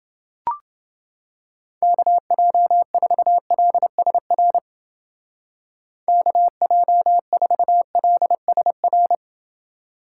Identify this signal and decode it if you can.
Morse code, heard as a keyed tone, about 30 wpm, 700 Hz. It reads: KJ4LSR KJ4LSR